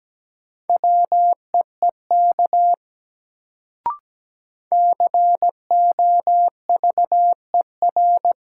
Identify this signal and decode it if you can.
Morse code, heard as a keyed tone, about 17 wpm, 700 Hz. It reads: WEEK COVER
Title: Sentences from Top 500 Words - New Word Drill - 17wpm